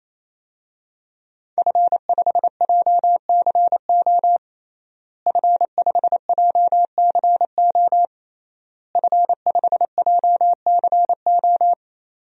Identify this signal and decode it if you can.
Morse code, heard as a keyed tone, about 28 wpm, 700 Hz. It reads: F5JCO F5JCO F5JCO
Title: DXCC-100 Call Signs - 5 Characters - Repeated 3x - 28wpm